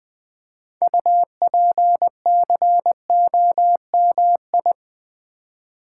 Morse code, heard as a keyed tone, about 20 wpm, 700 Hz. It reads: UPCOMI
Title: Build A Word - 3 Letters At A Time - 8-Letter Words - 20wpm